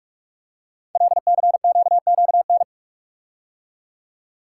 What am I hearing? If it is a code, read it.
Morse code, heard as a keyed tone, about 44 wpm, 700 Hz. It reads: LCQQN
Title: Code Groups - 5 Letters - 45wpm